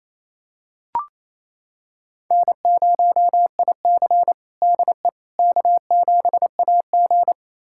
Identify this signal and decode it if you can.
Morse code, heard as a keyed tone, about 28 wpm, 700 Hz. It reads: N0IC DE K7AG
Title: USA Callsign DE Callsign - 4 Characters - 28wpm